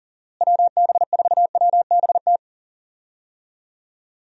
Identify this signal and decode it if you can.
Morse code, heard as a keyed tone, about 40 wpm, 700 Hz. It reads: WB4WBT
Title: USA Call Signs - 6 Characters - Repeated 3x - 40wpm